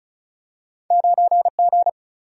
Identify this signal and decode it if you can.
Morse code, heard as a keyed tone, about 36 wpm, 700 Hz. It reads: 9 G
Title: Single-Letter Single-Number 35wpm